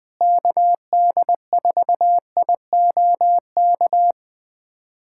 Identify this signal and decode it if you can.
Morse code, heard as a keyed tone, about 20 wpm, 700 Hz. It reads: KD4IOK
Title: USA Callsign DE Callsign 20wpm